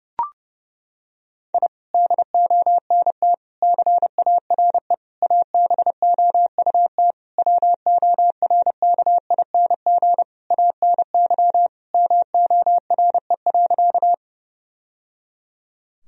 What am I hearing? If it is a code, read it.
Morse code, heard as a keyed tone, about 30 wpm, 700 Hz. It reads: I DONT CARE ABOUT WORKING ANY MORE.
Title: Sentences from Top 1000 Words - Review Part 4 - 30wpm